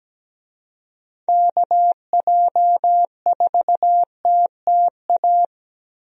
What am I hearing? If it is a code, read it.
Morse code, heard as a keyed tone, about 17 wpm, 700 Hz. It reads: KJ4TTA
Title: USA Call Signs - 6 Characters - Repeated 3x - 17wpm